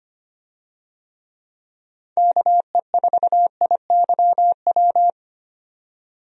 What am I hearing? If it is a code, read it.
Morse code, heard as a keyed tone, about 25 wpm, 700 Hz. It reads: KE4IYW
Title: USA Call Signs - 6 Characters - 25wpm